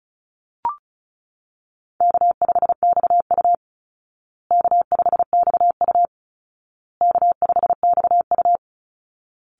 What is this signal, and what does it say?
Morse code, heard as a keyed tone, about 34 wpm, 700 Hz. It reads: K5XU K5XU K5XU
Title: Easy Call Signs Repeated 3x - 35wpm